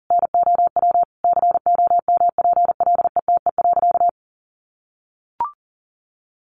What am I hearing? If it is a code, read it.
Morse code, heard as a keyed tone, about 40 wpm, 700 Hz. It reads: NOW COMPLETE.
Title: Sentences from Top 400 Words 40wpm